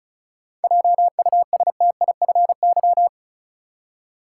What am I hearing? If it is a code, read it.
Morse code, heard as a keyed tone, about 36 wpm, 700 Hz. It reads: JUSTIFY
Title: Seven Letter Words - 35wpm